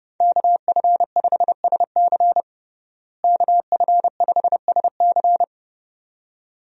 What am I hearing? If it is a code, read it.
Morse code, heard as a keyed tone, about 30 wpm, 700 Hz. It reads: KF5SC KF5SC
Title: USA Call Signs - 5 Characters - Repeated 2x - 30wpm